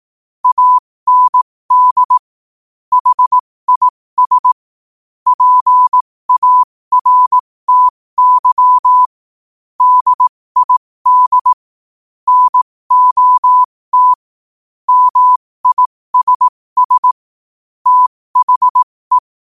Morse code, heard as a keyed tone, about 18 wpm, spaced as for 15 wpm, 1000 Hz. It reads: AND HIS PARTY DID NOT MISS THE
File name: qotdfiles / 15wpm